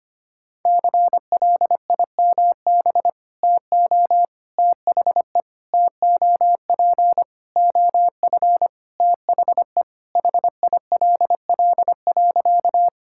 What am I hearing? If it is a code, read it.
Morse code, heard as a keyed tone, about 25 wpm, 700 Hz. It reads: CLIMB TO THE TOP OF THE HILL.
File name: practice / 25wpm / Sentences with 4-Word Phrases - 25wpm